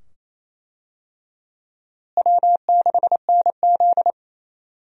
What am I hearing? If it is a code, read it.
Morse code, heard as a keyed tone, about 28 wpm, 700 Hz. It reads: W6NZ
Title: USA Call Signs - 4 Characters - Repeated 2x - 28wpm